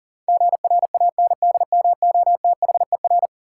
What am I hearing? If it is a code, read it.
Morse code, heard as a keyed tone, about 40 wpm, 700 Hz. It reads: GRANDMOTHER
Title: Sentences from Top 1200 Words - New Word Drill - 40wpm